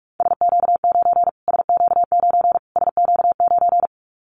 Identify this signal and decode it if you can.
Morse code, heard as a keyed tone, about 44 wpm, 700 Hz. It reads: SQ9 SQ9 SQ9
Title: Callsign Prefixes for Top 25 DXCC - Repeated 3x - 45wpm